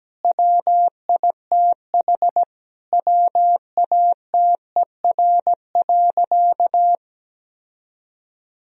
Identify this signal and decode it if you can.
Morse code, heard as a keyed tone, about 17 wpm, 700 Hz. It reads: WITH WATER.
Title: Sentences from Top 1000 Words - Review Part 3 - 17wpm